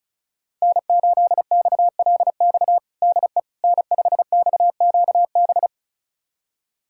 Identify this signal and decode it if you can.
Morse code, heard as a keyed tone, about 36 wpm, 700 Hz. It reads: N8XLX DE N5XQB
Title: USA Callsign DE Callsign - 5 Characters - 35wpm